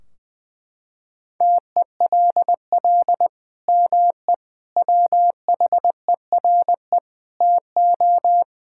Morse code, heard as a keyed tone, about 20 wpm, 700 Hz. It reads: TELL ME WHERE TO
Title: Sentences from Top 1600 Words 20wpm